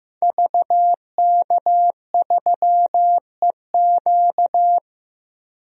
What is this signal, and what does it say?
Morse code, heard as a keyed tone, about 16 wpm, 700 Hz. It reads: VK3EQ
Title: DXCC-100 Call Signs - 5 Characters - Repeated 3x - 15wpm